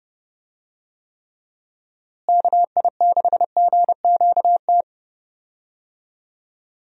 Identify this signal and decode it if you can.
Morse code, heard as a keyed tone, about 30 wpm, 700 Hz. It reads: KI6GQT
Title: USA Call Signs - 6 Characters - Repeated 2x - 30wpm